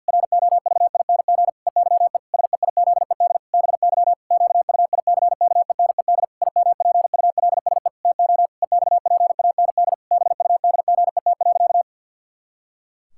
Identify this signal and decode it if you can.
Morse code, heard as a keyed tone, about 50 wpm, 700 Hz. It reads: ROVING EYE SEIZED BY QUICKENED IMPULSE TO EXPAND BUDGET.
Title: Pangram Challenge v2 - 50wpm